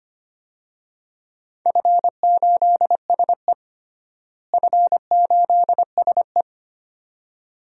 Morse code, heard as a keyed tone, about 25 wpm, 700 Hz. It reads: F8SE F8SE
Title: DXCC-100 Call Signs - 4 Characters - Repeated 2x - 25wpm